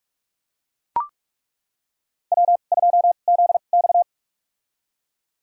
Morse code, heard as a keyed tone, about 42 wpm, spaced as for 31 wpm, 700 Hz. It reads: W2ZX
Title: USA Call Signs - 2x Character Spacing - 45wpm